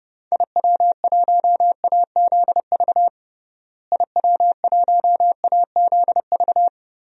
Morse code, heard as a keyed tone, about 30 wpm, 700 Hz. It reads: IW1AZV IW1AZV
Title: DXCC-100 Call Signs - 6 Characters - Repeated 3x - 30wpm